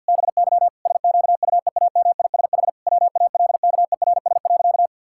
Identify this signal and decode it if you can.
Morse code, heard as a keyed tone, about 50 wpm, 700 Hz. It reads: BY SQUEAMISH WALKERS.
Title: Pangram Challenge v2 - 50wpm